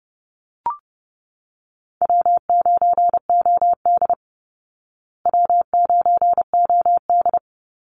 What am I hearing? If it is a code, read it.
Morse code, heard as a keyed tone, about 30 wpm, 700 Hz. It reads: W9OD W9OD
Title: Easy Call Signs Repeated 2x - 30wpm